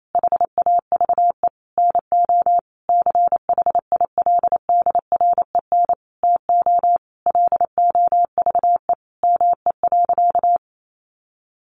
Morse code, heard as a keyed tone, about 28 wpm, 700 Hz. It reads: HAVE NO CHILDREN TO LOVE ME.
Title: Sentences from Top 400 Words 28wpm